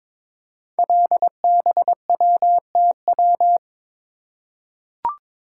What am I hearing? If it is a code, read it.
Morse code, heard as a keyed tone, about 22 wpm, 700 Hz. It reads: LBWTW
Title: Code Groups - 5 Letter:Numbers - 22wpm